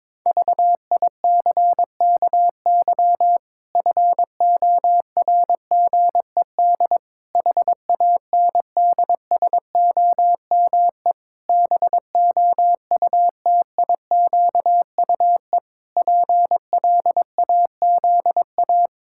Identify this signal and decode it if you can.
Morse code, heard as a keyed tone, about 22 wpm, 700 Hz. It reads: VICKY FORGED HANDSOME BOUTIQUE PLAZA
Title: Pangram Challenge v2 - 22wpm